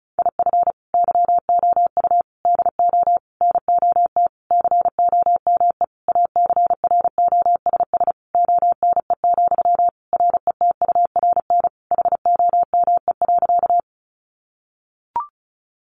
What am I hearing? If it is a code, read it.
Morse code, heard as a keyed tone, about 34 wpm, 700 Hz. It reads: IF YOU DO NOT COME ACROSS ONE, RETURN HOME.
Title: Sentences from Top 800 Words 35wpm